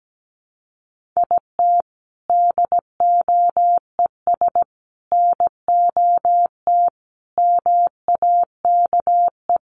Morse code, heard as a keyed tone, about 17 wpm, 700 Hz. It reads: IT DOES NOT MAKE
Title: Sets of 5 Words Encore2 17wpm